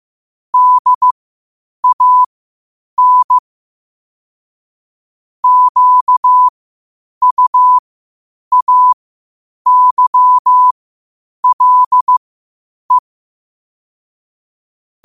Morse code, heard as a keyed tone, about 15 wpm, spaced as for 8 wpm, 1000 Hz. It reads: DAN QUAYLE
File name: qotdfiles / 8wpm